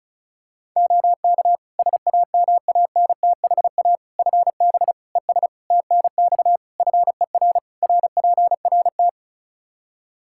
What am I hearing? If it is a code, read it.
Morse code, heard as a keyed tone, about 34 wpm, 700 Hz. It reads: OK SAMANTHA FB ES TNX FER RPRT
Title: Practice QSO - Exchange 2 - Info - 35wpm